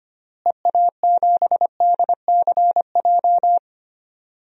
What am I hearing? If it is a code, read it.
Morse code, heard as a keyed tone, about 25 wpm, 700 Hz. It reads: EA7DCJ
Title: DXCC-100 Call Signs - 6 Characters - Repeated 2x - 25wpm